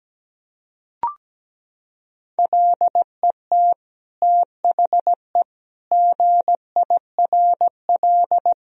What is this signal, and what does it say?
Morse code, heard as a keyed tone, about 17 wpm, 700 Hz. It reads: LET THE GIRL
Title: Sentences from Top 1000 Words - Review Part 1 - 17wpm